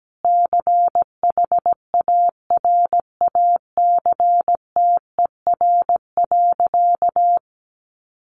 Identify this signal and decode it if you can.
Morse code, heard as a keyed tone, about 17 wpm, 700 Hz. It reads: CHARACTER.
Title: Sentences from Top 1200 Words 17wpm